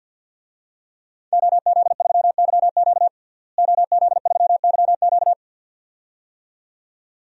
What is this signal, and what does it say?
Morse code, heard as a keyed tone, about 52 wpm, 700 Hz. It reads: OZ3YQ OZ3YQ
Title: DXCC-100 Call Signs - 5 Characters - Repeated 2x - 50wpm